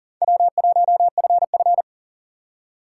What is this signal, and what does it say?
Morse code, heard as a keyed tone, about 40 wpm, 700 Hz. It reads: W1FF
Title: USA Call Signs - 6x Character Spacing - 40wpm